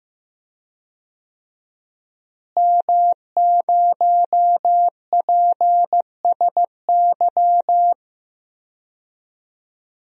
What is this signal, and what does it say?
Morse code, heard as a keyed tone, about 16 wpm, 700 Hz. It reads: M0PSY
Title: DXCC-100 Call Signs - 5 Characters - Repeated 2x - 15wpm